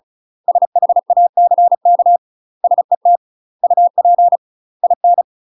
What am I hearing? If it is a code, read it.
Morse code, heard as a keyed tone, about 35 wpm, 700 Hz. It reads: SHACK SET UP IN
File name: UK MASTER QSOs TRAINING / 35WPM / UK MASTER QSOs 0014